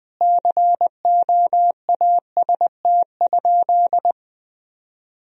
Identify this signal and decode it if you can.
Morse code, heard as a keyed tone, about 20 wpm, 700 Hz. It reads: COAST?
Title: Sentences from Top 1000 Words - Review Part 4 - 20wpm